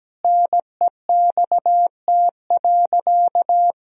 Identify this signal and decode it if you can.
Morse code, heard as a keyed tone, about 17 wpm, 700 Hz. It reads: NEXT.
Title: Sentences from Top 1000 Words - Review Part 2 - 17wpm